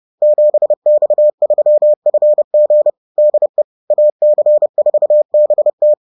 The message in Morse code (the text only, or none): 7X3FG DE AC4BT